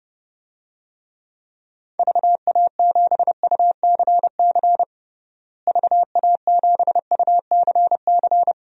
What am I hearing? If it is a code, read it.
Morse code, heard as a keyed tone, about 32 wpm, 700 Hz. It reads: VA7UCC VA7UCC
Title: DXCC-100 Call Signs - 6 Characters - Repeated 2x - 30wpm